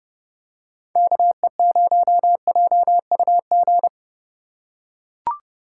Morse code, heard as a keyed tone, about 30 wpm, 700 Hz. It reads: KE0JUG
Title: USA Call Signs - 6 Characters - 3x Character Spacing - 30wpm